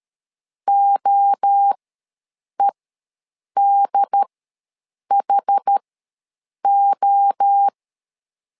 Morse code, heard as a keyed tone, about 13 wpm, spaced as for 7 wpm, 800 Hz. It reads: OEDHO
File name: LEZIONI / 3 MWRO / 3 MWRO ws 12wpm 1914